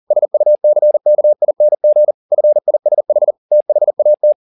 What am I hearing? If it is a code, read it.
Morse code, heard as a keyed tone, about 40 wpm, 600 Hz. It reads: SUCKING FISH THAT
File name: the old man and the sea 2 40wpm